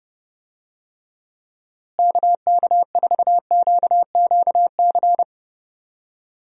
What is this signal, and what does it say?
Morse code, heard as a keyed tone, about 30 wpm, 700 Hz. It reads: KK4QQC
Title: USA Call Signs - 6 Characters - Repeated 3x - 30wpm